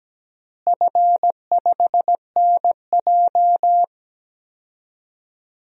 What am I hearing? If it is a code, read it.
Morse code, heard as a keyed tone, about 17 wpm, 700 Hz. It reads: F5NJ
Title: DXCC-100 Call Signs - 4 Characters - Repeated 3x - 17wpm